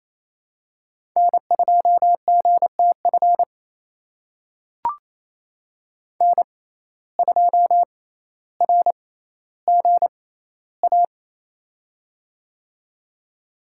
Morse code, heard as a keyed tone, about 28 wpm, spaced as for 14 wpm, 700 Hz. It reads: N2GTF N2RGA
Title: USA Call Signs - 5 Characters - 6x Character Spacing - 28wpm